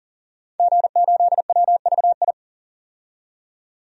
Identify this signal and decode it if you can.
Morse code, heard as a keyed tone, about 40 wpm, 700 Hz. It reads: G8WVI
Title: DXCC-100 Call Signs - 5 Characters - 40wpm